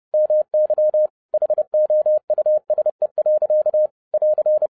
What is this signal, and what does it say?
Morse code, heard as a keyed tone, about 30 wpm, 610 Hz. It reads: MY HOUSE. <AR>
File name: 2011-07-mp3 / n55 BACK FROM A BIG ACCIDENT 30wpm